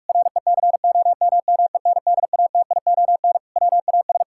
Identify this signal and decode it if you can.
Morse code, heard as a keyed tone, about 41 wpm, 700 Hz. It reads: RECOMMENDATION WAS
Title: Sentences from Top 1600 Words 45wpm